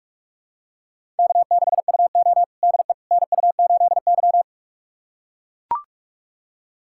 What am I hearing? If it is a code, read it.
Morse code, heard as a keyed tone, about 43 wpm, 700 Hz. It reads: K6UO DE NU8Y
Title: USA Callsign DE Callsign - 4 Characters - 45wpm